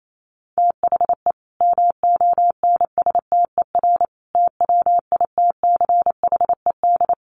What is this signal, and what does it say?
Morse code, heard as a keyed tone, about 28 wpm, 700 Hz. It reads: THE MONSTER TWITCHED
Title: A Sound Of Thunder 28wpm